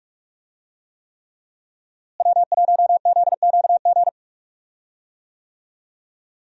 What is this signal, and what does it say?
Morse code, heard as a keyed tone, about 46 wpm, 700 Hz. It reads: W1ZQG
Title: USA Call Signs - 5 Characters - Repeated 3x - 45wpm